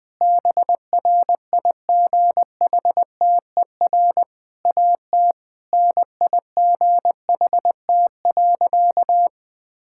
Morse code, heard as a keyed tone, about 20 wpm, 700 Hz. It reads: BRIGHTER AT NIGHT.
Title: Difficult Sentences with Hints - 20wpm